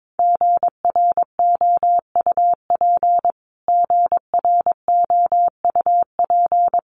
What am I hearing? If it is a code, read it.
Morse code, heard as a keyed tone, about 22 wpm, 700 Hz. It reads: GROUP GROUP
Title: Top 300 Words Repeated 22wpm